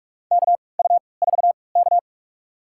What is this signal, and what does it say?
Morse code, heard as a keyed tone, about 48 wpm, spaced as for 26 wpm, 700 Hz. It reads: KU4K
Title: USA Call Signs - 4 Characters - 3x Character Spacing - 45wpm